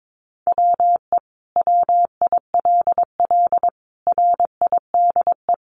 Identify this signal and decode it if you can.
Morse code, heard as a keyed tone, about 22 wpm, 700 Hz. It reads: WE WILL RIDE
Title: Sentences from Top 1200 Words 22wpm